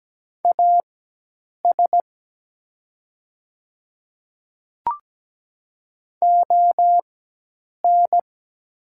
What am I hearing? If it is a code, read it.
Morse code, heard as a keyed tone, about 17 wpm, spaced as for 8 wpm, 700 Hz. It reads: AS ON
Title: 4x Character Spacing - Top 100 Words - 17wpm